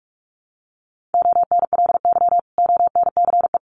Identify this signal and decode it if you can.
Morse code, heard as a keyed tone, about 44 wpm, 700 Hz. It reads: ONLY ONCE
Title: Sentences from Top 1100 Words 45wpm